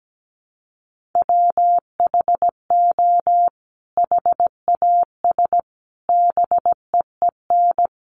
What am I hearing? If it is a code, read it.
Morse code, heard as a keyed tone, about 17 wpm, 700 Hz. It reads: WHO HAS BEEN
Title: Sentences from Top 1100 Words 17wpm